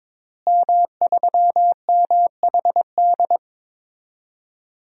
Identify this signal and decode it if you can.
Morse code, heard as a keyed tone, about 22 wpm, 700 Hz. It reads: M3MHD
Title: DXCC-100 Call Signs - 5 Characters - Repeated 3x - 22wpm